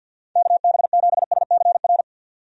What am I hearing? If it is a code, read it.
Morse code, heard as a keyed tone, about 51 wpm, 700 Hz. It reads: KB7SCR
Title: USA Call Signs - 6 Characters - 50wpm